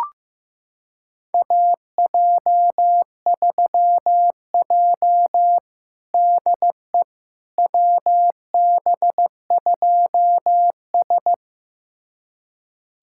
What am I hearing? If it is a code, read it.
Morse code, heard as a keyed tone, about 15 wpm, 700 Hz. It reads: AJ3J DE WB2S
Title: USA Callsign DE Callsign - 4 Characters - 15wpm